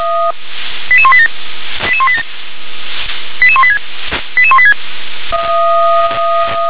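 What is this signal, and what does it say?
Five-tone selective call (ZVEI), heard as quick runs of five tones, tones of about 70 ms. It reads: 88187 88187 88186 88186
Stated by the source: ZVEI five-tone selective call received on an SDR.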